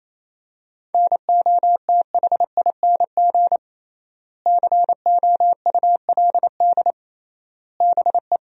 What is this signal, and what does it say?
Morse code, heard as a keyed tone, about 28 wpm, 700 Hz. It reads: NOTHING COULD BE
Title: 3x Word-Spacing - Sentences from Top 500 Words - 28wpm